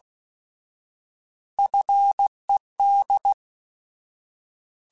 Morse code, heard as a keyed tone, about 16 wpm, 800 Hz. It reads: FED
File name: Alphabetical / Alphabetical mp3s / Fs @15